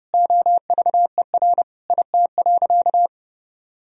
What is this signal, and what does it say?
Morse code, heard as a keyed tone, about 32 wpm, 700 Hz. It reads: OVER IT.
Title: Sentences from Top 1000 Words - Review Part 3 - 30wpm